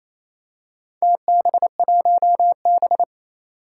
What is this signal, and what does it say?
Morse code, heard as a keyed tone, about 28 wpm, 700 Hz. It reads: TB1B
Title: Four Challenging Letter-Number Combinations 28wpm